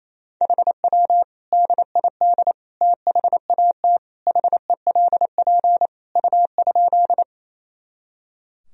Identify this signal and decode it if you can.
Morse code, heard as a keyed tone, about 27 wpm, 700 Hz. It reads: HW DID THAT HELP U?
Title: Sentences from Top 200 Words and Abbreviations 28wpm